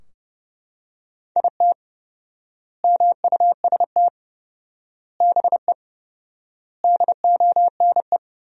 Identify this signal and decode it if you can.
Morse code, heard as a keyed tone, about 30 wpm, 700 Hz. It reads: IT MUST BE DONE
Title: 4x Word-Spacing - Sentences from Top 500 Words - 30wpm